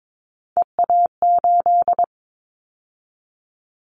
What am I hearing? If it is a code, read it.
Morse code, heard as a keyed tone, about 22 wpm, 700 Hz. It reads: EA8
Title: Callsign Prefixes for Top 100 DXCC - 22wpm